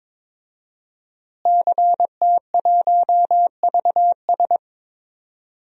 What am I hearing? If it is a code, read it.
Morse code, heard as a keyed tone, about 22 wpm, 700 Hz. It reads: CT1VS
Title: DXCC-100 Call Signs - 5 Characters - Repeated 2x - 22wpm